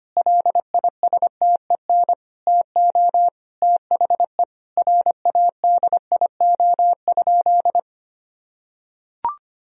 Morse code, heard as a keyed tone, about 25 wpm, 700 Hz. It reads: LISTEN TO THE RADIO?
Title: Sentences from Top 1000 Words - Review Part 3 - 25wpm